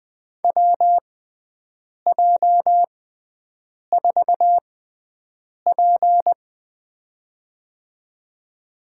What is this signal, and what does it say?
Morse code, heard as a keyed tone, about 20 wpm, spaced as for 7 wpm, 700 Hz. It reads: WJ4P
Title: USA Call Signs - 4 Characters - 6x Character Spacing - 20wpm